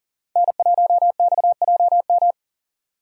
Morse code, heard as a keyed tone, about 40 wpm, 700 Hz. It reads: N1XJM
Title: K1USN SST QSO Practice - 40wpm